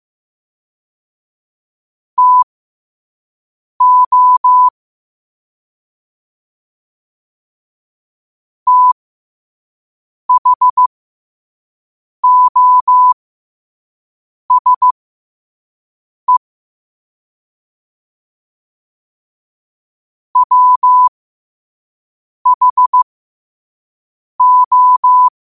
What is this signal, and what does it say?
Morse code, heard as a keyed tone, about 15 wpm, spaced as for 5 wpm, 1000 Hz. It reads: TO THOSE WHO
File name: qotdfiles / 5wpm